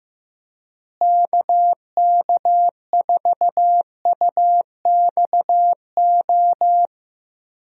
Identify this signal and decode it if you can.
Morse code, heard as a keyed tone, about 15 wpm, 700 Hz. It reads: KK4UXO
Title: USA Call Signs - 6 Characters - Repeated 3x - 15wpm